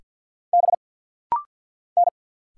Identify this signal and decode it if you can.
Morse code, heard as a keyed tone, about 50 wpm, 690 Hz. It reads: B N
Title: Single Letters - Rapid-Fire 50wpm